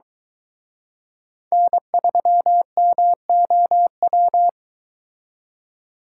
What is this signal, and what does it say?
Morse code, heard as a keyed tone, about 23 wpm, 700 Hz. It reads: N3MOW